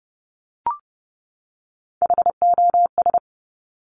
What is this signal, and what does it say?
Morse code, heard as a keyed tone, about 30 wpm, 690 Hz. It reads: HOS
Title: Top Three Letter Combinations in Words 30wpm